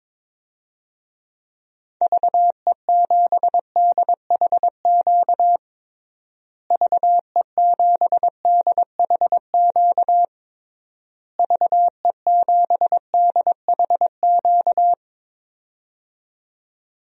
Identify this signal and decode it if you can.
Morse code, heard as a keyed tone, about 22 wpm, 700 Hz. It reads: VE7DHQ VE7DHQ VE7DHQ
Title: DXCC-100 Call Signs - 6 Characters - Repeated 3x - 22wpm